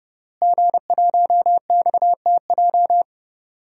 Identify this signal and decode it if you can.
Morse code, heard as a keyed tone, about 30 wpm, 700 Hz. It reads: G1XTJ
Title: DXCC-100 Call Signs - 5 Characters - Repeated 3x - 30wpm